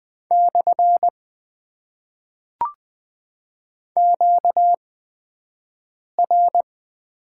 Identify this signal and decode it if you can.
Morse code, heard as a keyed tone, about 20 wpm, spaced as for 5 wpm, 700 Hz. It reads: / QR
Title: Top 90 QSO Elements - 8x Character Spacing - 20wpm